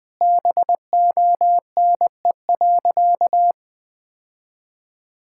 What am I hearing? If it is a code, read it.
Morse code, heard as a keyed tone, about 20 wpm, 700 Hz. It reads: BONE.
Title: Sentences from Top 1600 Words 20wpm